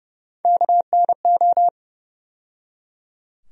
Morse code, heard as a keyed tone, about 32 wpm, 700 Hz. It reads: KNO
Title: USA Call Sign Postfixes - 3 Letters 30wpm